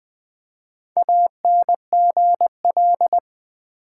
Morse code, heard as a keyed tone, about 20 wpm, 700 Hz. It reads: ANGL
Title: Build A Word - 1 Letter At A Time - 5-Letter Words - 20wpm